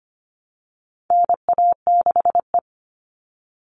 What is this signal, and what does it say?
Morse code, heard as a keyed tone, about 25 wpm, 700 Hz. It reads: NA6E
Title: All Cut Numbers - 4-Digit 25wpm